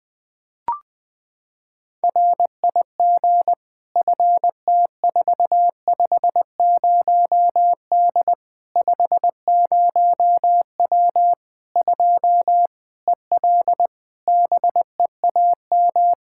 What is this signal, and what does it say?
Morse code, heard as a keyed tone, about 21 wpm, 700 Hz. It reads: RIG FT450D 50W 2 EL BEAM
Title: Practice QSO - Exchange 2 - Info - 20wpm